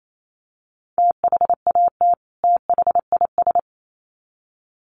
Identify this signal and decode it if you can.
Morse code, heard as a keyed tone, about 28 wpm, 700 Hz. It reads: THAT THIS
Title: Sets of 2 Words 28wpm